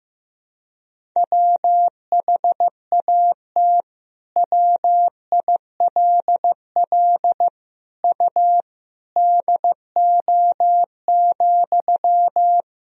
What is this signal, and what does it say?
Morse code, heard as a keyed tone, about 15 wpm, 700 Hz. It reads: WHAT WILL U DO,
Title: Sentences from Top 100 Words and Abbreviations 15wpm